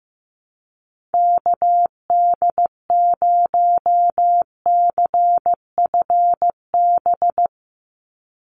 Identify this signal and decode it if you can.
Morse code, heard as a keyed tone, about 15 wpm, 700 Hz. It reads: KD0CFB
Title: USA Call Signs 15wpm